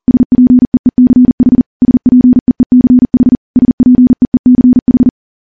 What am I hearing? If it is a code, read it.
Morse code, heard as a keyed tone, about 40 wpm, 250 Hz. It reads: SPEECH SPEECH SPEECH
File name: morse / morse-training-40WPM